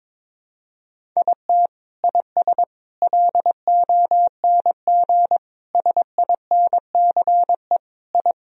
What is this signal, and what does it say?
Morse code, heard as a keyed tone, about 22 wpm, 700 Hz. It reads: IT IS LONG SINCE I
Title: Sentences from Top 1000 Words - Review Part 1 - 22wpm